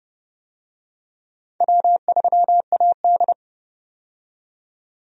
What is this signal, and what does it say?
Morse code, heard as a keyed tone, about 31 wpm, 700 Hz. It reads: W3AD
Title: USA Call Signs - 4 Characters - 30wpm